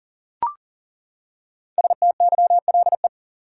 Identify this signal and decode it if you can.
Morse code, heard as a keyed tone, about 37 wpm, 700 Hz. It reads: STYLE
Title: Five Letter Words - 40wpm